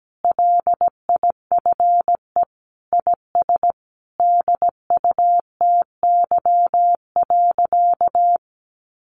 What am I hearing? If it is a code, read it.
Morse code, heard as a keyed tone, about 17 wpm, 700 Hz. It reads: LIFE IS DUTY.
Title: Sentences from Top 1500 Words 17wpm